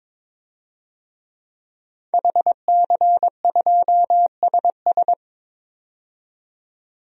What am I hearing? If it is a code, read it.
Morse code, heard as a keyed tone, about 22 wpm, 700 Hz. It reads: HC2SS
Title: DXCC-100 Call Signs - 5 Characters - 22wpm